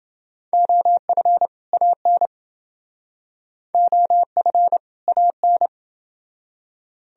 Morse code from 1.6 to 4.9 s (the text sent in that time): AN OF